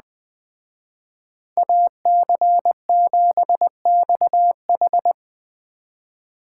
Morse code, heard as a keyed tone, about 20 wpm, 700 Hz. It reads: AC7XH